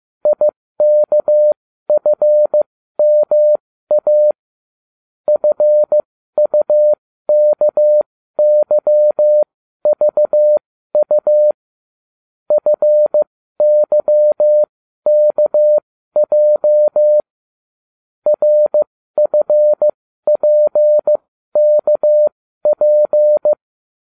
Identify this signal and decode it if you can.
Morse code, heard as a keyed tone, about 15 wpm, spaced as for 12 wpm, 600 Hz. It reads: IKFMA FUKYVU FYKJ RFPKP